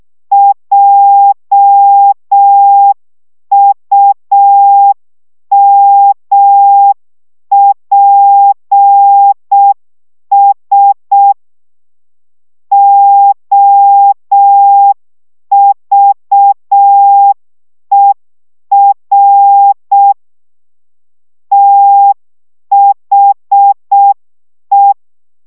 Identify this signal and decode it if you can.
Morse code, heard as a keyed tone, about 6 wpm, 800 Hz. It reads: JUMPS OVER THE
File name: szamolo / CW 5 baud